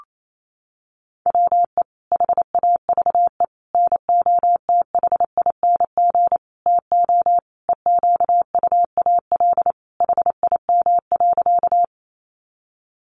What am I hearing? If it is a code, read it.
Morse code, heard as a keyed tone, about 28 wpm, 700 Hz. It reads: WE HAVE NOTHING TO EQUAL HIM.
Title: Sentences from Top 1400 Words 28wpm